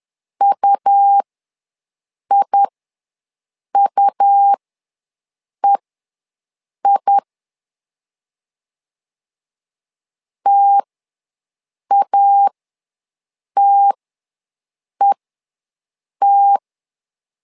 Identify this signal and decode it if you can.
Morse code, heard as a keyed tone, about 11 wpm, spaced as for 6 wpm, 800 Hz. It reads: UIUEI TATET
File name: LEZIONI / 2 UHDN / 2 U ws 10wpm 2017